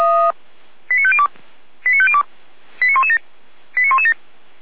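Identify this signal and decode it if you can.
Five-tone selective call (ZVEI), heard as quick runs of five tones, tones of about 70 ms. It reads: 89502 89502 89117 89117
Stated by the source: ZVEI five-tone selective call 89092, received on an SDR.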